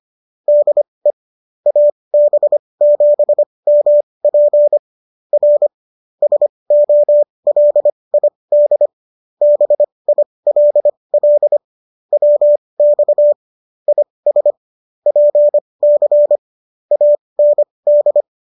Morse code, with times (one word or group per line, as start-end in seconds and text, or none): 0.5-1.1 DE
1.7-4.8 AB7MP
5.3-5.7 R
6.2-8.9 SOLID
9.4-11.6 BILL
12.1-13.3 WX
13.9-14.5 IS
15.1-16.4 PC
16.9-18.2 AND